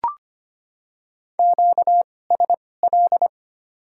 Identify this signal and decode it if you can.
Morse code, heard as a keyed tone, about 25 wpm, spaced as for 18 wpm, 700 Hz. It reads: QSL
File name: Top 90 QSO Elements - 2x Character Spacing - 25wpm